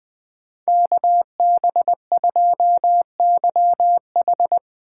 Morse code, heard as a keyed tone, about 23 wpm, 700 Hz. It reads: KB2YH
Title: Speed Racing - USA 2x2 Call Signs 20wpm